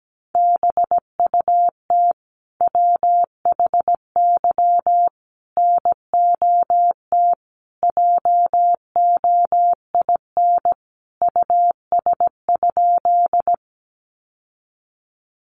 Morse code, heard as a keyed tone, about 17 wpm, 700 Hz. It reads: BUT WHY NOT JOIN US?
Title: Sentences from Top 1400 Words 17wpm